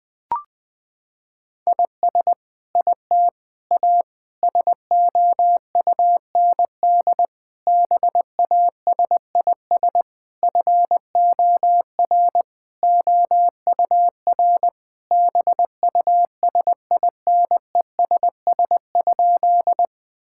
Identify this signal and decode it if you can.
Morse code, heard as a keyed tone, about 20 wpm, 700 Hz. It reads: IS IT A SOUND BASIS FOR OUR BUSINESS?
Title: Sentences from Top 1000 Words - Review Part 5 - 20wpm